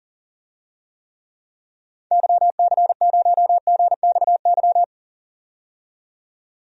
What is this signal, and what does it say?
Morse code, heard as a keyed tone, about 40 wpm, 700 Hz. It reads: YC0GXY
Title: DXCC-100 Call Signs - 6 Characters - 40wpm